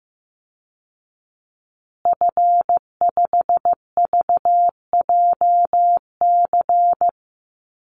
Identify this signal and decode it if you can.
Morse code, heard as a keyed tone, about 14 wpm, 700 Hz. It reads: F5VJC
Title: Call Signs Repeated 3x - 15wpm